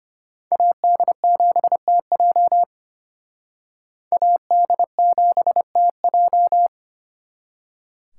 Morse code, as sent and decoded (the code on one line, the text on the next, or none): .- -.. --... - .--- / .- -.. --... - .---
AD7TJ AD7TJ